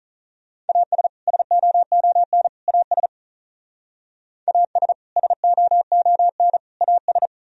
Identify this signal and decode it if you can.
Morse code, heard as a keyed tone, about 44 wpm, 700 Hz. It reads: AS SOON AS AS SOON AS